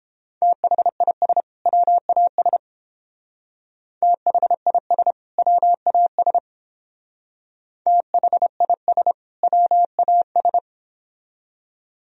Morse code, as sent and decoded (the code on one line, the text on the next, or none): - .... .. ... / .-- .- ... / - .... .. ... / .-- .- ... / - .... .. ... / .-- .- ...
THIS WAS THIS WAS THIS WAS